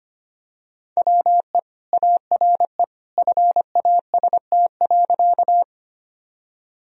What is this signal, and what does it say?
Morse code, heard as a keyed tone, about 25 wpm, 700 Hz. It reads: WE ARE FAST.
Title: Sentences from Top 1000 Words - Review Part 2 - 25wpm